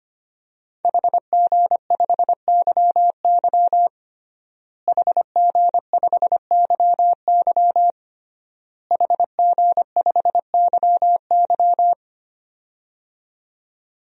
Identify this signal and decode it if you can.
Morse code, heard as a keyed tone, about 25 wpm, 700 Hz. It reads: HG5YY HG5YY HG5YY
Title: DXCC-100 Call Signs - 5 Characters - Repeated 3x - 25wpm